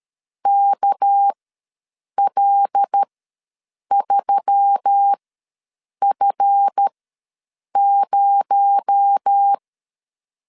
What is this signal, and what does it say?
Morse code, heard as a keyed tone, about 13 wpm, spaced as for 7 wpm, 800 Hz. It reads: KL3F0
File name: LEZIONI / 7 GF9Z / 7 GF9 ws 12wpm 3344